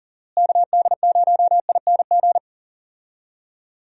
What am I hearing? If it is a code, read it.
Morse code, heard as a keyed tone, about 37 wpm, 700 Hz. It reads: KD0ING
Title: USA Call Signs - 6 Characters - Repeated 2x - 40wpm